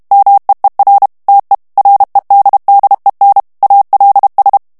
Morse code, heard as a keyed tone, about 32 wpm, 800 Hz. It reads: MEER TE REDDEN ALS
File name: HOLLANDAIS texte clair 2 vitesse 30 mots minute